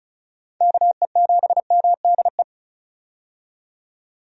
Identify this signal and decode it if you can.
Morse code, heard as a keyed tone, about 36 wpm, 700 Hz. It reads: KE7MDE
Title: USA Call Signs - 6 Characters - 35wpm